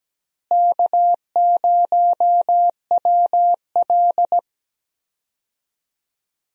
Morse code, heard as a keyed tone, about 17 wpm, 700 Hz. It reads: K0WL
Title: USA Call Signs - 4 Characters - Repeated 3x - 17wpm